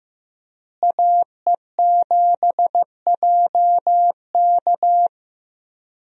Morse code, heard as a keyed tone, about 15 wpm, 700 Hz. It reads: AE7JK
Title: USA Call Signs - 5 Characters - Repeated 2x - 15wpm